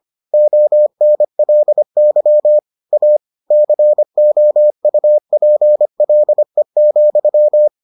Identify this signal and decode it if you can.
Morse code, heard as a keyed tone, about 25 wpm, 600 Hz. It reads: ONLY A COUPLE,